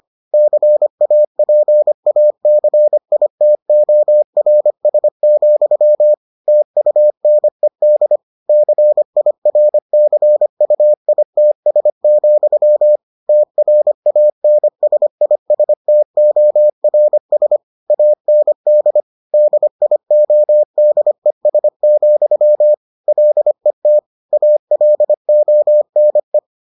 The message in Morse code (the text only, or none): CAPACITORS, TUNED CIRCUITS, TRANSISTORS AND DIODES, LET ALONE